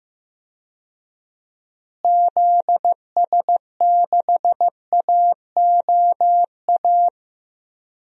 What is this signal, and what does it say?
Morse code, heard as a keyed tone, about 15 wpm, 700 Hz. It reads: ZS6AOA
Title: DXCC-100 Call Signs - 6 Characters - Repeated 3x - 15wpm